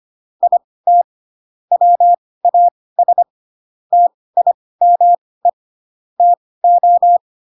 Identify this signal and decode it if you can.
Morse code, heard as a keyed tone, about 25 wpm, spaced as for 18 wpm, 700 Hz. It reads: IT WAS TIME TO
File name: Pinocchio ch3 12 25 ik1hge